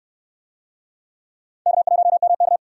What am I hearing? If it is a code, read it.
Morse code, heard as a keyed tone, about 65 wpm, 700 Hz. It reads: K1MK